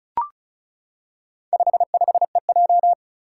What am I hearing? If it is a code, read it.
Morse code, heard as a keyed tone, about 36 wpm, 700 Hz. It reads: 55EJ
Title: Four Challenging Letter-Number Combinations 35wpm